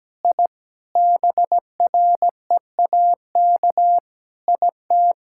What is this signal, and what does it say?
Morse code, heard as a keyed tone, about 17 wpm, 700 Hz. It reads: I BREAK IT
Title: Sentences from Top 1000 Words - Review Part 6 - 17wpm